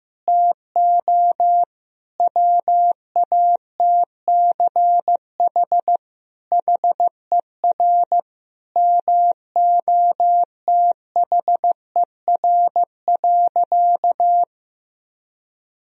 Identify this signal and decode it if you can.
Morse code, heard as a keyed tone, about 15 wpm, 700 Hz. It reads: TO WATCH HER MOTHER.
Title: Sentences from Top 1000 Words - Review Part 1 - 15wpm